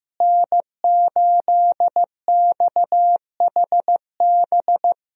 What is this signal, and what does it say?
Morse code, heard as a keyed tone, about 15 wpm, 700 Hz. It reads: N8XHB
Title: USA Call Signs - 5 Characters - Repeated 2x - 15wpm